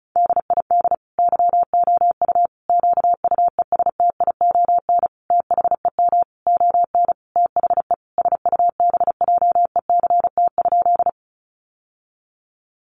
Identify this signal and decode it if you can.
Morse code, heard as a keyed tone, about 34 wpm, 700 Hz. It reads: DID YOU QUESTION THEM ON THE SUBJECT?
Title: Sentences from Top 800 Words 35wpm